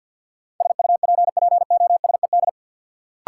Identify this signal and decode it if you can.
Morse code, heard as a keyed tone, about 53 wpm, 700 Hz. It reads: SUPPOSED
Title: Sets of 3 Words - New Word Drill - 50wpm